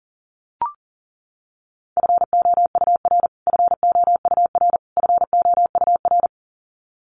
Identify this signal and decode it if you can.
Morse code, heard as a keyed tone, about 37 wpm, 700 Hz. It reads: FOUR FOUR FOUR
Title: Top 300 Words Repeated 40wpm